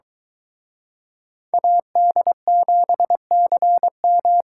Morse code, heard as a keyed tone, about 23 wpm, 700 Hz. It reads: AD7CM